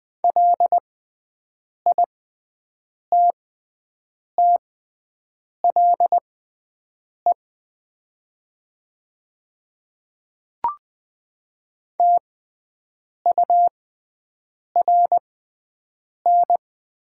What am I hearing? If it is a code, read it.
Morse code, heard as a keyed tone, about 21 wpm, spaced as for 7 wpm, 700 Hz. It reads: LITTLE TURN
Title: 6x Character Spacing - Top 200 Words - 20wpm